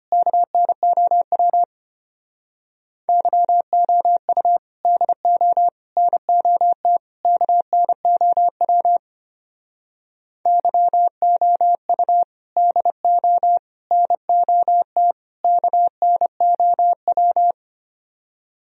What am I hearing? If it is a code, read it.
Morse code, heard as a keyed tone, about 30 wpm, 700 Hz. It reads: KNOW YOU DO NOT KNOW YOU DO NOT KNOW
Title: Speed-Racing - Sets of 4 Words 25wpm